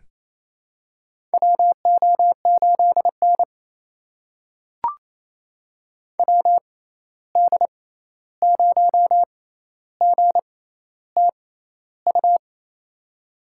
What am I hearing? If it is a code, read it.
Morse code, heard as a keyed tone, about 28 wpm, spaced as for 13 wpm, 700 Hz. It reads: WO8N WD0GTU
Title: USA Call Signs - 6x Character Spacing - 28wpm